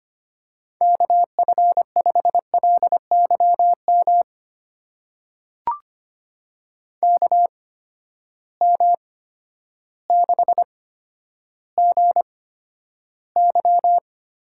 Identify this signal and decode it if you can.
Morse code, heard as a keyed tone, about 25 wpm, spaced as for 11 wpm, 700 Hz. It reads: KF5LYM KM6GY
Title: USA Call Signs - 6 Characters - 8x Character Spacing - 25wpm